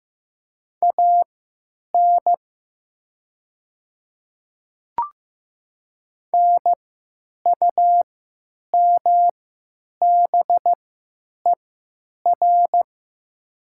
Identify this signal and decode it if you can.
Morse code, heard as a keyed tone, about 15 wpm, spaced as for 9 wpm, 700 Hz. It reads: AN NUMBER
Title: 3x Character Spacing - Top 100 Words - 15wpm